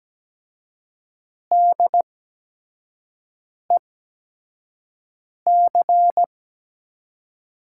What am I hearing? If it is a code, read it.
Morse code, heard as a keyed tone, about 17 wpm, spaced as for 5 wpm, 700 Hz. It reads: DEC
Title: USA Call Signs - 5 Characters - 8x Character Spacing - 17wpm